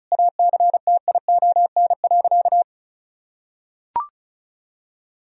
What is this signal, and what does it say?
Morse code, heard as a keyed tone, about 34 wpm, 700 Hz. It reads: ACTION.
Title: Sentences from Top 1000 Words - Review Part 4 - 35wpm